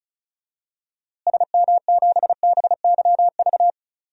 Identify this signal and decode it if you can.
Morse code, heard as a keyed tone, about 36 wpm, 700 Hz. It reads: SM7BYV
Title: DXCC-100 Call Signs - 6 Characters - Repeated 3x - 35wpm